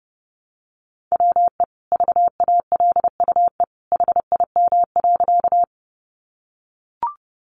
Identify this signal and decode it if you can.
Morse code, heard as a keyed tone, about 30 wpm, 700 Hz. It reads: WE VALUE HIM.
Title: Sentences from Top 600 Words 30wpm